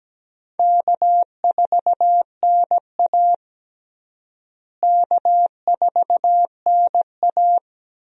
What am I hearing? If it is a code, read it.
Morse code, heard as a keyed tone, about 17 wpm, 700 Hz. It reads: K4NA K4NA
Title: USA Call Signs - 4 Characters - Repeated 2x - 17wpm